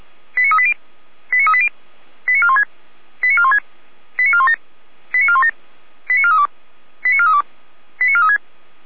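Five-tone selective call (ZVEI), heard as quick runs of five tones, tones of about 70 ms. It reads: 89380 89380 89416 89416 89417 89417 89432 89432 89436
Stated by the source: ZVEI five-tone selective call received on SDR, from the Feuerwehr Orlamünde fire brigade server.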